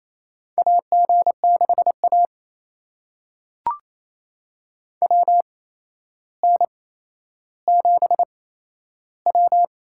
Morse code, heard as a keyed tone, about 27 wpm, spaced as for 12 wpm, 700 Hz. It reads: AG6A WN7W
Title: USA Call Signs - 4 Characters - 8x Character Spacing - 28wpm